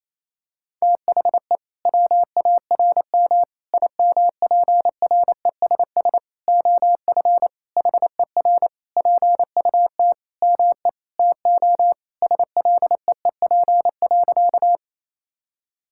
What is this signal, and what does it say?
Morse code, heard as a keyed tone, about 28 wpm, 700 Hz. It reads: THE WARM IMPRESS OF HER PUT ME TO SLEEP.
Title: Sentences from Top 1600 Words 28wpm